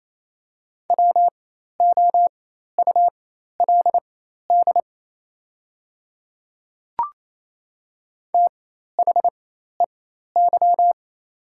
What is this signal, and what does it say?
Morse code, heard as a keyed tone, about 28 wpm, spaced as for 13 wpm, 700 Hz. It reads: WOULD THEY
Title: 4x Character Spacing - Top 200 Words - 28wpm